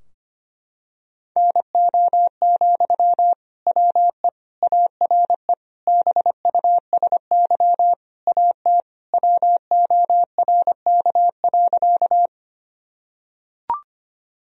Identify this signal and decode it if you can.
Morse code, heard as a keyed tone, about 25 wpm, 700 Hz. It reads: NO, WE ARE BUSY AT WORK.
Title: Sentences from Top 1000 Words - Review Part 3 - 25wpm